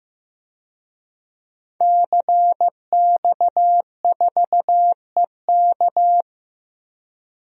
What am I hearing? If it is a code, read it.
Morse code, heard as a keyed tone, about 15 wpm, 700 Hz. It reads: CX4EK
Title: DXCC-100 Call Signs - 5 Characters - 15wpm